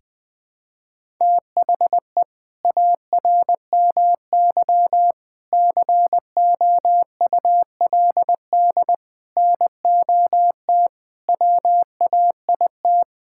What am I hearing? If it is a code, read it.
Morse code, heard as a keyed tone, about 21 wpm, 700 Hz. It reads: THE ARMY COULD NOT WAIT
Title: Sentences from Top 1000 Words - Review Part 5 - 20wpm